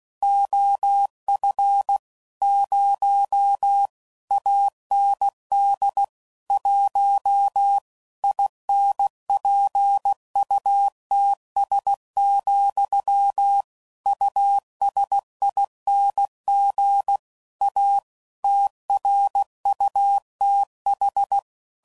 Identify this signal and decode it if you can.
Morse code, heard as a keyed tone, about 16 wpm, 800 Hz. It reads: OF 0 AND 1 INPUTS, USING A TRUTH
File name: CW Code Groups @15WPM / Technical Subjects / Gates Intro